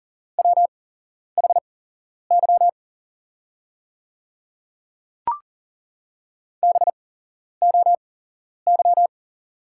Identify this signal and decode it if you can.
Morse code, heard as a keyed tone, about 40 wpm, spaced as for 11 wpm, 700 Hz. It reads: WHY BOY
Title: 8x Character Spacing - Top 300 Words - 40wpm